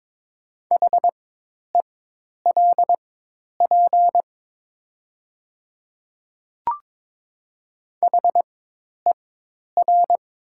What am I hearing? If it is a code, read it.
Morse code, heard as a keyed tone, about 23 wpm, spaced as for 10 wpm, 690 Hz. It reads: HELP HER
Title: 4x Character Spacing - Top 300 Words - 22wpm